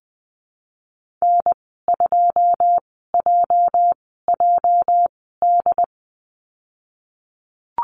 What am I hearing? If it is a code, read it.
Morse code, heard as a keyed tone, about 20 wpm, spaced as for 15 wpm, 700 Hz. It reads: N2JJD
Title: 219 Lesson 35g - Number 0 - Call Signs 20wpm